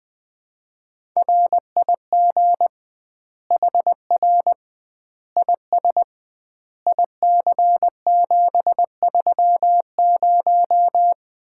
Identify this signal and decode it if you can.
Morse code, heard as a keyed tone, about 20 wpm, 700 Hz. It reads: RIG HR IS IC730
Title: Common QSO Phrases - 2x Word-Spacing - 20wpm